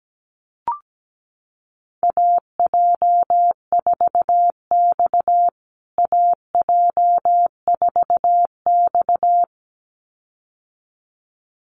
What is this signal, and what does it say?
Morse code, heard as a keyed tone, about 17 wpm, 700 Hz. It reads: AJ4X AJ4X
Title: USA Call Signs Repeated 2x 17wpm